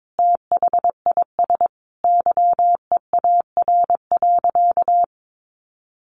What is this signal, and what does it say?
Morse code, heard as a keyed tone, about 22 wpm, 700 Hz. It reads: THIS YEAR.
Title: Sentences from Top 1500 Words 22wpm